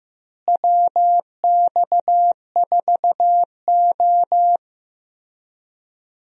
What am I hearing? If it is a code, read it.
Morse code, heard as a keyed tone, about 15 wpm, 700 Hz. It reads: WX4O
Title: USA Call Signs - 4 Characters - Repeated 2x - 15wpm